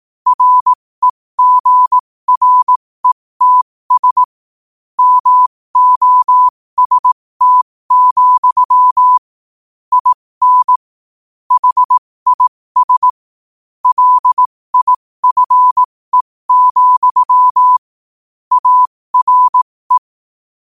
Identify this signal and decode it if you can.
Morse code, heard as a keyed tone, about 18 wpm, spaced as for 15 wpm, 1000 Hz. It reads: REGRETS MOST, IN HIS LIFE, ARE
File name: qotdfiles / 15wpm